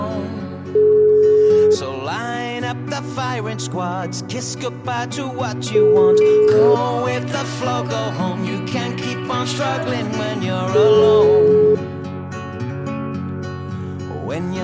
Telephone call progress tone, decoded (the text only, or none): ringback tone